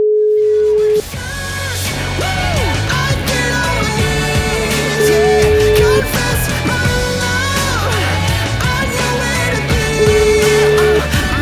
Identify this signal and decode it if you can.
Ringback tone (Europe, 425 Hz), with noise.